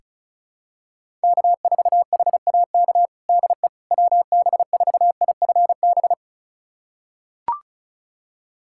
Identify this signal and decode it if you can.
Morse code, heard as a keyed tone, about 36 wpm, 700 Hz. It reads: K4HAK DE WB4IFB
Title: USA Callsign DE Callsign 35wpm